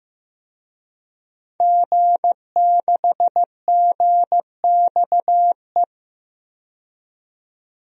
Morse code, heard as a keyed tone, about 16 wpm, 700 Hz. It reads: G6GXE
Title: DXCC-100 Call Signs - 5 Characters - 15wpm